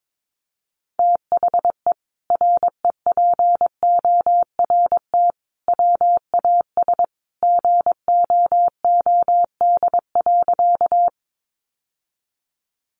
Morse code, heard as a keyed tone, about 22 wpm, 700 Hz. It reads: THE REPORT WAS GOOD.
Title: Sentences from Top 900 Words 22wpm